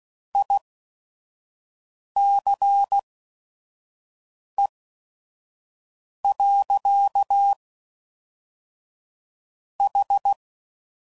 Morse code, heard as a keyed tone, about 16 wpm, spaced as for 5 wpm, 800 Hz. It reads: ICE. H
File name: CW Code Groups @5WPM / QSOs / QSO WD0FBT